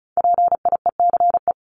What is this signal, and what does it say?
Morse code, heard as a keyed tone, about 34 wpm, 700 Hz. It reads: PIECE
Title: Top 500 Words 35wpm